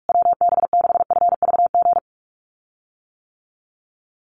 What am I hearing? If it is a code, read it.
Morse code, heard as a keyed tone, about 44 wpm, 700 Hz. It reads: WB6FVG
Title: USA Call Signs Repeated 3x 45wpm